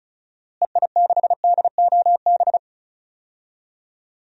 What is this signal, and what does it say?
Morse code, heard as a keyed tone, about 36 wpm, 700 Hz. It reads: EI6DOB
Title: DXCC-100 Call Signs - 6 Characters - Repeated 3x - 35wpm